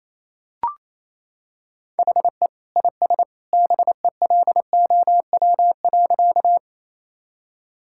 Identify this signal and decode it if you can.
Morse code, heard as a keyed tone, about 28 wpm, 700 Hz. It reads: HE IS BELOW.
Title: Sentences from Top 1000 Words - Review Part 5 - 28wpm